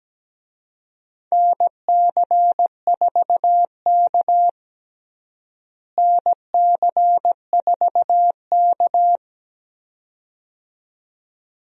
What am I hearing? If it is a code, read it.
Morse code, heard as a keyed tone, about 17 wpm, 700 Hz. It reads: NC4K NC4K
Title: USA Call Signs - 4 Characters - Repeated 2x - 17wpm